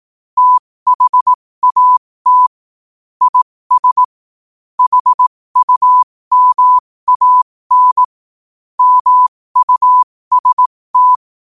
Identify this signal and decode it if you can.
Morse code, heard as a keyed tone, about 18 wpm, spaced as for 15 wpm, 1000 Hz. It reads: THAT IS HUMAN MUST
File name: 15wpm